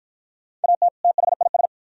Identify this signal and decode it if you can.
Morse code, heard as a keyed tone, about 50 wpm, 700 Hz. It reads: AT THIS